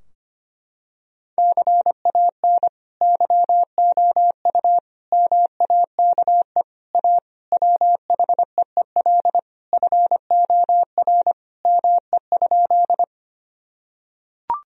Morse code, heard as a keyed tone, about 25 wpm, 700 Hz. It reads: CAN YOU MAKE A WHEEL FOR ME?
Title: Sentences from Top 1000 Words - Review Part 3 - 25wpm